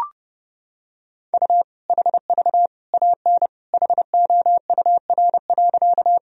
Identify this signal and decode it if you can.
Morse code, heard as a keyed tone, about 30 wpm, 700 Hz. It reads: U HV AN HOUR.
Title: Sentences from Top 400 Words and Abbreviations 30wpm